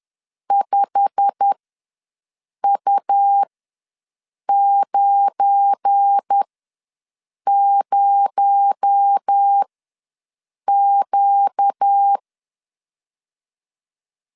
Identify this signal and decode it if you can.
Morse code, heard as a keyed tone, about 10 wpm, spaced as for 6 wpm, 800 Hz. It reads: 5U90Q
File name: LEZIONI / 8 YJ45 / 8 YJ45 ws 10wpm 3843